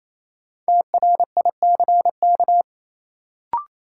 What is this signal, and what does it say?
Morse code, heard as a keyed tone, about 28 wpm, 700 Hz. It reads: TRICK
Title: Sentences from Top 1500 Words - New Word Drill - 28wpm